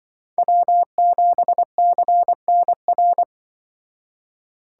Morse code, heard as a keyed tone, about 23 wpm, 700 Hz. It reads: W7CNR
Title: Speed Racing - USA 1x3 Call Signs 20wpm